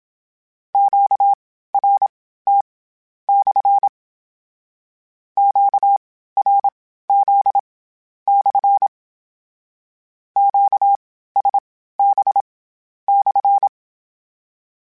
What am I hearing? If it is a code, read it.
Morse code, heard as a keyed tone, about 27 wpm, spaced as for 13 wpm, 800 Hz. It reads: QRT/ QRZ/ QSB/
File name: Q Calls / Q Calls-mp3s / Q Calls SPSL1 @25